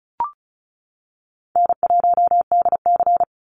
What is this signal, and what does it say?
Morse code, heard as a keyed tone, about 36 wpm, 700 Hz. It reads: N1DC
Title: Call Signs - 35wpm